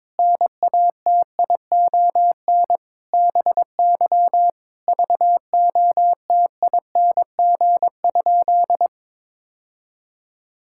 Morse code, heard as a keyed tone, about 22 wpm, 700 Hz. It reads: NATION BY VOTING?
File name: Sentences with 9-Word Phrases - 22wpm